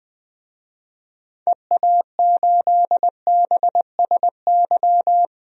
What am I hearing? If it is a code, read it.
Morse code, heard as a keyed tone, about 20 wpm, 700 Hz. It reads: EA8BSY
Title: DXCC-100 Call Signs - 6 Characters - Repeated 2x - 20wpm